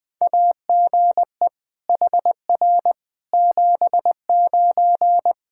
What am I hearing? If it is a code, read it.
Morse code, heard as a keyed tone, about 20 wpm, 700 Hz. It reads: AGE HR 79
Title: K1USN SST QSO Practice 20wpm